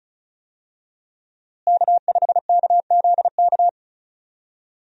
Morse code, heard as a keyed tone, about 34 wpm, 700 Hz. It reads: K5KZK
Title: USA Call Signs - 5 Characters - Repeated 3x - 35wpm